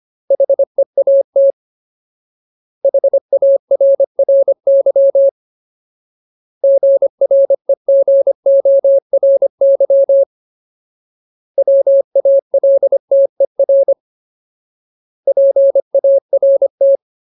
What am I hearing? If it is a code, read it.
Morse code, heard as a keyed tone, about 25 wpm, 550 Hz. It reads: HEAT HARRY GREGORY WALTER PART
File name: Ultimate Word List / 25WPM Ultimate